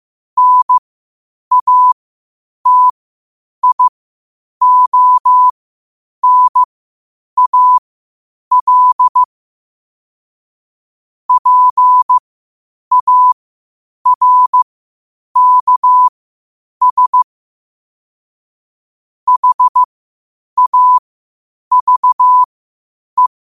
Morse code, heard as a keyed tone, about 15 wpm, spaced as for 8 wpm, 1000 Hz. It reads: NATIONAL PARKS HAVE
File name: qotdfiles / 8wpm